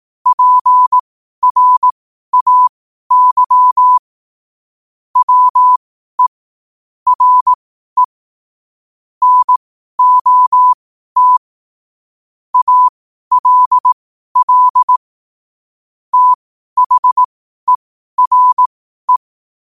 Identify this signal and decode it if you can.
Morse code, heard as a keyed tone, about 18 wpm, spaced as for 12 wpm, 1000 Hz. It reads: PRAY WE RE NOT ALL THERE
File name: 2026-01-07-12wpm